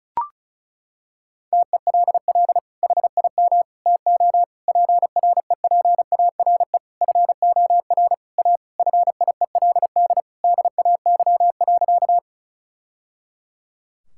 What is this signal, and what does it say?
Morse code, heard as a keyed tone, about 34 wpm, 700 Hz. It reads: TELL HIM TO PREPARE FOR A FIELD DAY.
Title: Sentences from Top 1000 Words - Review Part 6 - 35wpm